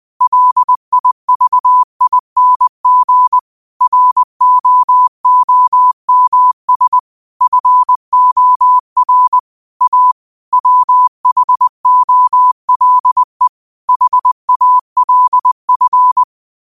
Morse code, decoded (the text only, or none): LIVING ROOMS FOR A WHOLE HALF